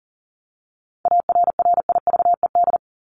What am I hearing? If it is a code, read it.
Morse code, heard as a keyed tone, about 37 wpm, 700 Hz. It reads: ARRIVED
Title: ED Suffix - Top 100 Words with Base Words 40wpm